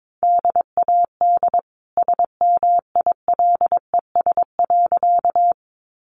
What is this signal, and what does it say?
Morse code, heard as a keyed tone, about 22 wpm, 700 Hz. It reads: DAD SMILES.
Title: Easy Phrases 22wpm